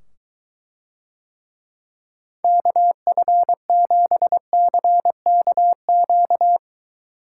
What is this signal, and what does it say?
Morse code, heard as a keyed tone, about 23 wpm, 700 Hz. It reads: KF7CKQ